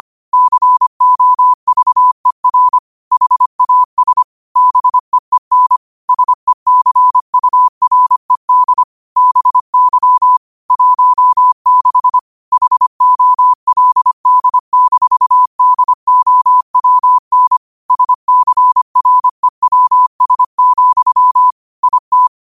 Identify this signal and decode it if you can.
Morse code, heard as a keyed tone, about 25 wpm, 1000 Hz. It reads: COVER HAS BEEN SECURED BY 16 HOLD-DOWN SCREWS, IT
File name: qotdfiles / 25wpm